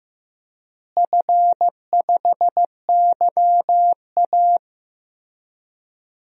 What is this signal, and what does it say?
Morse code, heard as a keyed tone, about 15 wpm, 700 Hz. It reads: F5YA
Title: DXCC-100 Call Signs - 4 Characters - Repeated 3x - 15wpm